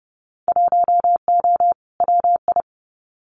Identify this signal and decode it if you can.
Morse code, heard as a keyed tone, about 30 wpm, 700 Hz. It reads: 1O WI
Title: Winter Field Day Exchanges Encore 30wpm